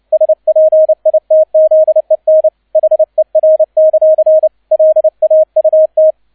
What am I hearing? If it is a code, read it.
Morse code, heard as a keyed tone, about 29 wpm, 630 Hz, with radio noise. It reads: SPITZEN HER; LAUT
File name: 002 Zweiter Streich 30wpm